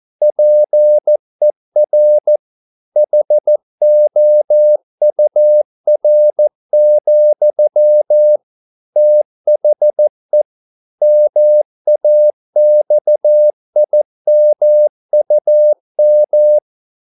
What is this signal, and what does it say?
Morse code, heard as a keyed tone, about 14 wpm, 600 Hz. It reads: PER HOUR, THE MAXIMUM